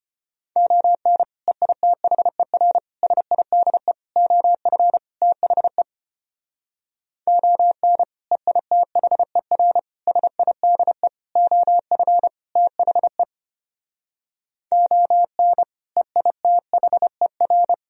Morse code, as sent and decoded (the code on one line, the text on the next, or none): --- -. / . .. - .... .-. / ... .. -.. / --- ..-. / - .... / --- -. / . .. - .... .-. / ... .. -.. / --- ..-. / - .... / --- -. / . .. - .... .-.
ON EITHER SIDE OF THE ON EITHER SIDE OF THE ON EITHER